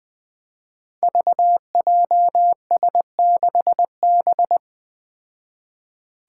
Morse code, heard as a keyed tone, about 21 wpm, 700 Hz. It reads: VJS6B
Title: Five Challenging Letter-Number Combinations 20wpm